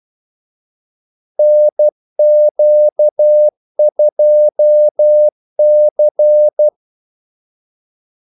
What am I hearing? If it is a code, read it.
Morse code, heard as a keyed tone, about 12 wpm, 600 Hz. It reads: NQ2C
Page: 2 ekstra ordmellemrum mellem kaldesignalerne